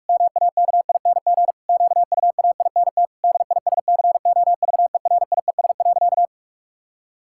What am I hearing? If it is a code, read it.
Morse code, heard as a keyed tone, about 46 wpm, 700 Hz. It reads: MAKING QUAINT DISCOVERIES.
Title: Pangram Challenge v2 - 45wpm